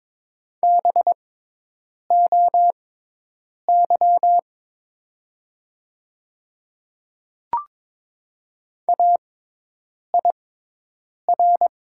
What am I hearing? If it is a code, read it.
Morse code, heard as a keyed tone, about 22 wpm, spaced as for 8 wpm, 700 Hz. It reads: BOY AIR
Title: 6x Character Spacing - Top 300 Words - 22wpm